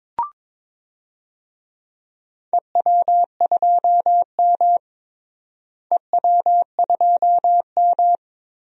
Morse code, heard as a keyed tone, about 22 wpm, 700 Hz. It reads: EW2M EW2M
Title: DXCC-100 Call Signs - 4 Characters - Repeated 2x - 22wpm